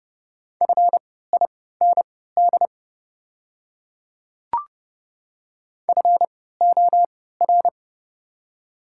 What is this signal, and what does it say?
Morse code, heard as a keyed tone, about 30 wpm, spaced as for 17 wpm, 700 Hz. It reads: FIND FOR
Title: 3x Character Spacing - Top 100 Words - 30wpm